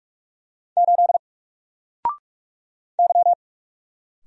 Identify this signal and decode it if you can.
Morse code, heard as a keyed tone, about 44 wpm, 700 Hz. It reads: 8 Y
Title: Single Letter-Number 45wpm